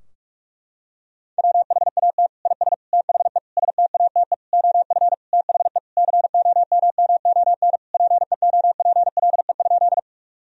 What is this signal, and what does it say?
Morse code, heard as a keyed tone, about 46 wpm, 700 Hz. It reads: WHAT IS THE STATE OF THE COMMON PEOPLE?
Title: Sentences from Top 1000 Words - Review Part 3 - 45wpm